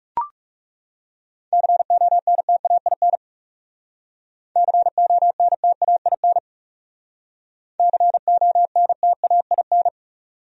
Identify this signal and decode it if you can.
Morse code, heard as a keyed tone, about 36 wpm, 700 Hz. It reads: CONTAIN CONTAIN CONTAIN
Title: Speed-Racing Top 500 Words 30wpm